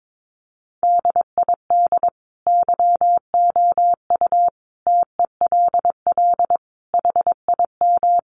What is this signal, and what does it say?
Morse code, heard as a keyed tone, about 22 wpm, 700 Hz. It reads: DID YOU TELL HIM
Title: Sentences from Top 400 Words 22wpm